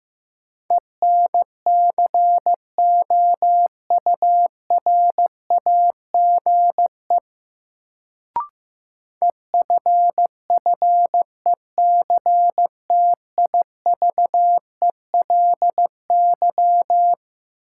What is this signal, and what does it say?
Morse code, heard as a keyed tone, about 15 wpm, 700 Hz. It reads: ENCOURAGE EFFECTIVELY
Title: Sentences from Top 1400 Words - New Word Drill - 15wpm